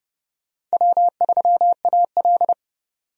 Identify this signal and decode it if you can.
Morse code, heard as a keyed tone, about 30 wpm, 700 Hz. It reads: W3AL
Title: USA Call Signs - 4 Characters - 30wpm